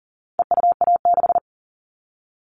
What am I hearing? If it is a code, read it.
Morse code, heard as a keyed tone, about 39 wpm, 700 Hz. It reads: EUA6
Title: All Cut Numbers - 4-Digit 40wpm